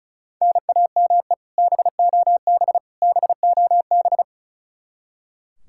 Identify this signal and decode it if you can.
Morse code, heard as a keyed tone, about 34 wpm, 700 Hz. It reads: NAME BOB BOB
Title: Practice QSO - Exchange 1 - Beginning - 35wpm